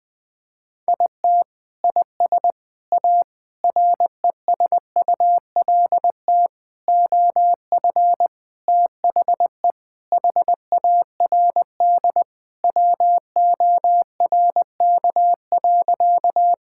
Morse code, heard as a keyed tone, about 20 wpm, 700 Hz. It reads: IT IS A RESULT OF THE HARD WORK.
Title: Sentences with 4-Word Phrases - Spoken Beginning - 20wpm